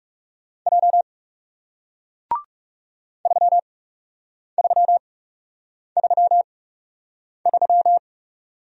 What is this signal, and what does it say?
Morse code, heard as a keyed tone, about 41 wpm, spaced as for 8 wpm, 700 Hz. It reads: J 3333